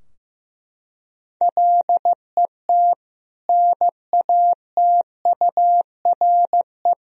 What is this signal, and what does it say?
Morse code, heard as a keyed tone, about 15 wpm, 700 Hz. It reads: LET NATURE
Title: Sentences from Top 1000 Words - Review Part 3 - 15wpm